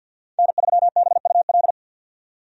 Morse code, heard as a keyed tone, about 53 wpm, 700 Hz. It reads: N3BUL
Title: USA Call Signs - 5 Characters - 50wpm